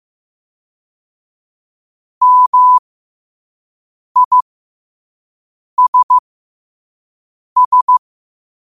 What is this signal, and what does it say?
Morse code, heard as a keyed tone, about 15 wpm, spaced as for 5 wpm, 1000 Hz. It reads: MISS
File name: qotdfiles / 5wpm